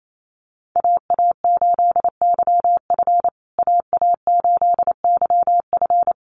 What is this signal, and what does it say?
Morse code, heard as a keyed tone, about 28 wpm, 700 Hz. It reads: AA8YF AA8YF
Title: USA Call Signs Repeated 3x 28wpm